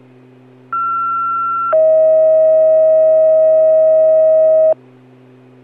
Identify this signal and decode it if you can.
Two-tone page, heard as a one-second tone, then a three-second tone, about 1360 then 640 Hz. It is heard from 0.7 to 4.7 s.